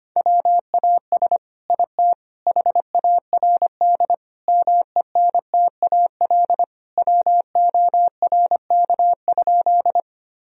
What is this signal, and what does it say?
Morse code, heard as a keyed tone, about 25 wpm, 700 Hz. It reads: WAS IT HARD MENTAL WORK?
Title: Sentences from Top 1000 Words - Review Part 4 - 25wpm